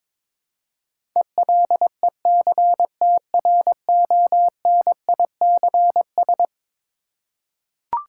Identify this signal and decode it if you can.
Morse code, heard as a keyed tone, about 22 wpm, 700 Hz. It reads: ELECTRONICS
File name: practice / 22wpm / Eleven Letter Words - 22wpm